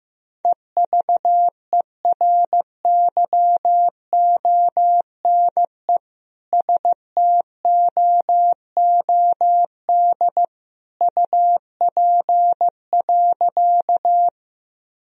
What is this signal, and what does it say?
Morse code, heard as a keyed tone, about 15 wpm, 700 Hz. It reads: EVERYONE STOOD UP.
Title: Sentences from Top 1000 Words - Review Part 2 - 15wpm